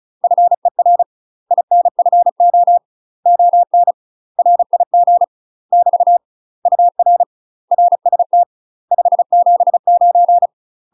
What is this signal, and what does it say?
Morse code, heard as a keyed tone, about 34 wpm, 700 Hz. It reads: FER INFO ON RIG <BT> UR RST 579
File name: UK MASTER QSOs TRAINING / 35WPM / UK MASTER QSOs 0010